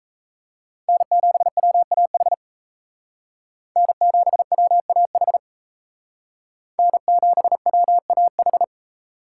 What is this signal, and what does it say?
Morse code, heard as a keyed tone, about 37 wpm, 700 Hz. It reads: N7WAH N7WAH N7WAH